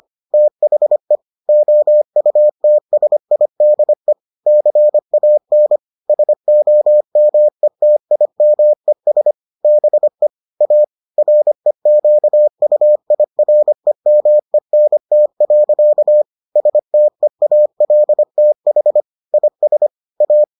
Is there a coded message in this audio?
Yes: Morse code from 0.3 to 20.4 s, heard as a keyed tone, about 25 wpm, 600 Hz. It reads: THE OUTSIDE CAN SOMETIMES BE A REQUIREMENT. STEALTH IS A